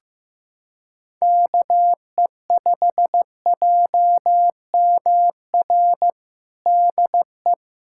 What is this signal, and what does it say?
Morse code, heard as a keyed tone, about 16 wpm, 700 Hz. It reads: KE5JMR DE
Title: USA Callsign DE Callsign 15wpm